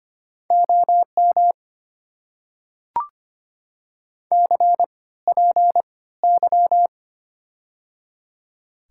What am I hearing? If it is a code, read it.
Morse code, heard as a keyed tone, about 25 wpm, spaced as for 16 wpm, 690 Hz. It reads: OM CPY
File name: Top 90 QSO Elements - 3x Character Spacing - 25wpm